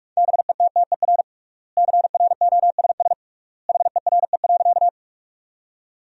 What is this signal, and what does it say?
Morse code, heard as a keyed tone, about 46 wpm, 700 Hz. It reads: BETTER CROSS HERE.
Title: 3x Word-Spacing - Sentences from Top 400 Words - 45wpm